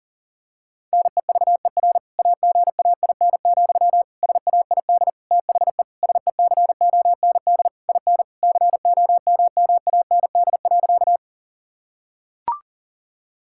Morse code, heard as a keyed tone, about 40 wpm, 700 Hz. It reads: NEVER AGAIN, SAID THE SECOND IN COMMAND.
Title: Sentences from Top 1600 Words 40wpm